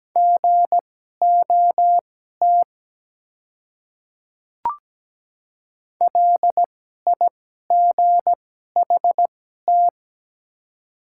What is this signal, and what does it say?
Morse code, heard as a keyed tone, about 17 wpm, spaced as for 12 wpm, 700 Hz. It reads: GOT LIGHT
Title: 2x Character Spacing - Top 300 Words - 17wpm